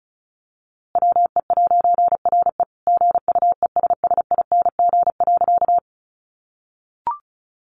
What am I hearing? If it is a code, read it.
Morse code, heard as a keyed tone, about 34 wpm, 700 Hz. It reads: WE'RE GUESSING.
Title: A Sound Of Thunder 35wpm